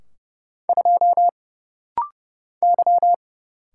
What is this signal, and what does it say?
Morse code, heard as a keyed tone, about 30 wpm, 700 Hz. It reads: 2 Y
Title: Single Letter-Number - Rapid-Fire 30wpm